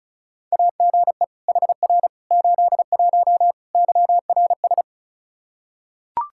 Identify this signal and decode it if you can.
Morse code, heard as a keyed tone, about 36 wpm, 700 Hz. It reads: AGE HR 81 YRS
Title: Practice QSO - Exchange 3 - More Info - 35wpm